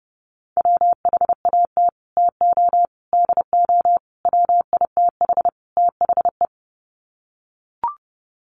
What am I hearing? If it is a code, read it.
Morse code, heard as a keyed tone, about 30 wpm, 700 Hz. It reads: WHAT TO DO WITH THE
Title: Sets of 5 Words Encore2 30wpm